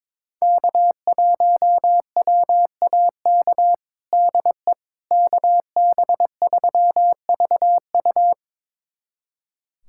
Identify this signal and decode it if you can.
Morse code, heard as a keyed tone, about 22 wpm, 700 Hz. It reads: K1WAK DE KB3VU
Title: USA Callsign DE Callsign - 5 Characters - 22wpm